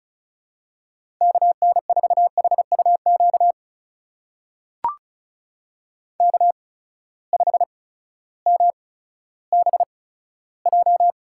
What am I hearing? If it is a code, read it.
Morse code, heard as a keyed tone, about 36 wpm, spaced as for 16 wpm, 700 Hz. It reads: KN4HUQ K5MBJ
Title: USA Call Signs - 8x Character Spacing - 35wpm